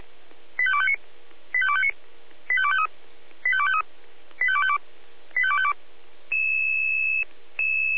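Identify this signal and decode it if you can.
Five-tone selective call (ZVEI), heard as quick runs of five tones, tones of about 70 ms. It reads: 85259 85259 85253 85253 85252 85252
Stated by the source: ZVEI five-tone selective call received on SDR.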